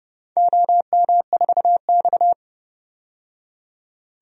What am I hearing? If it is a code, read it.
Morse code, heard as a keyed tone, about 32 wpm, 700 Hz. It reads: OM4X
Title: DXCC-100 Call Signs - 4 Characters - 30wpm